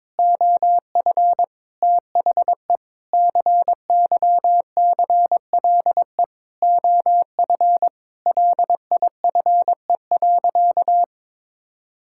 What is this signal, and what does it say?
Morse code, heard as a keyed tone, about 22 wpm, 700 Hz. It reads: OF THE CYCLE OF LIFE.
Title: Sentences from Top 1000 Words - Review Part 6 - 22wpm